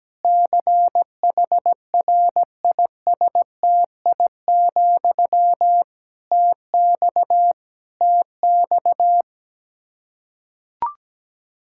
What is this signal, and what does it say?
Morse code, heard as a keyed tone, about 17 wpm, 700 Hz. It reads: CHRISTI, TX TX
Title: K1USN SST QSO Practice 17wpm